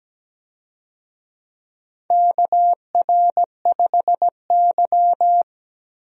Morse code, heard as a keyed tone, about 17 wpm, 700 Hz. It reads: KR5Y
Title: USA Call Signs - 4 Characters - Repeated 2x - 17wpm